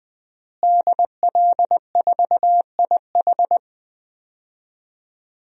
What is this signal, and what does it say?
Morse code, heard as a keyed tone, about 20 wpm, 700 Hz. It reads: DL4IH
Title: DXCC-100 Call Signs - 5 Characters - Repeated 3x - 20wpm